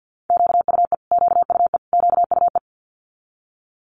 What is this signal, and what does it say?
Morse code, heard as a keyed tone, about 50 wpm, 700 Hz. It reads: QUE QUE QUE
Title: Multi-letter Phonemes Repeated 3x 50wpm